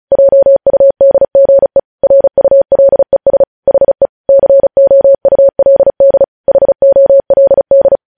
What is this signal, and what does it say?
Morse code, heard as a keyed tone, about 36 wpm, 550 Hz. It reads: JUDGE RULES HE COULD HOLD
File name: podcast / News headlines-1x-35x35